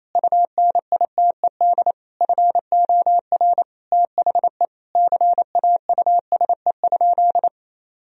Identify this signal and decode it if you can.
Morse code, heard as a keyed tone, about 28 wpm, 700 Hz. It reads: UNITED FOR THE CAUSE?
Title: Sentences from Top 1000 Words - Review Part 4 - 28wpm